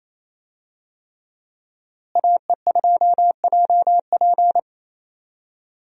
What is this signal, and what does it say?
Morse code, heard as a keyed tone, about 27 wpm, 700 Hz. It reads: AE2JP
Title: USA Call Signs - 5 Characters - Repeated 3x - 28wpm